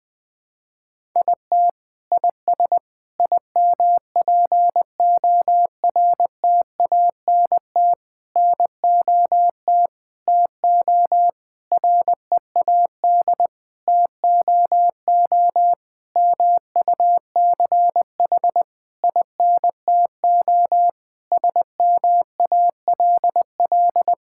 Morse code, heard as a keyed tone, about 21 wpm, 700 Hz. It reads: IT IS IMPORTANT NOT TO READ TOO MUCH INTO SMALL
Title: Sentences with 9-Word Phrases - 20wpm